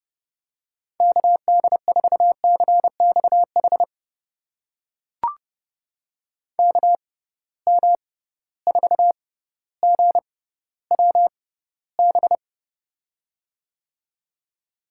Morse code, heard as a keyed tone, about 32 wpm, spaced as for 16 wpm, 700 Hz. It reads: KD4CXH KM4GWB
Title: USA Call Signs - 6 Characters - 6x Character Spacing - 30wpm